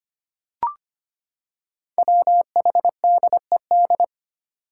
Morse code, heard as a keyed tone, about 25 wpm, 700 Hz. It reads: WHDED
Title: Five Challenging Letter-Number Combinations 25wpm